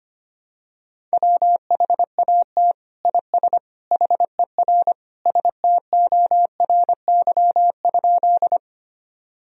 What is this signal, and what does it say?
Morse code, heard as a keyed tone, about 25 wpm, 700 Hz. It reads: WHAT IS HER STORY?
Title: Sentences from Top 1000 Words - Review Part 1 - 25wpm